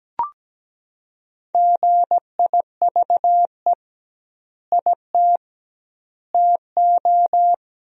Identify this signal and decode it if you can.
Morse code, heard as a keyed tone, about 17 wpm, 700 Hz. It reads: GIVE IT TO
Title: 2x Word-Spacing - Sentences from Top 200 Words - 17wpm